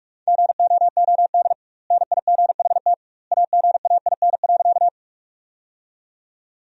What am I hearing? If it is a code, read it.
Morse code, heard as a keyed tone, about 46 wpm, 700 Hz. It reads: GOOD NIGHT AGAIN.
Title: 2x Word-Spacing - Sentences from Top 300 Words - 45wpm